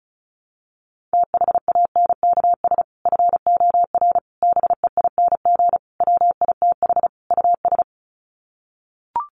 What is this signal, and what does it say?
Morse code, heard as a keyed tone, about 34 wpm, 700 Hz. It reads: THANKS FOR BEING WITH US
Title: Sets of 5 Words 35wpm